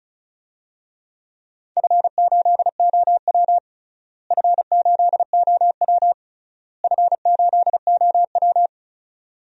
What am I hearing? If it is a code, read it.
Morse code, heard as a keyed tone, about 34 wpm, 700 Hz. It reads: F8OW F8OW F8OW
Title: DXCC-100 Call Signs - 4 Characters - Repeated 3x - 35wpm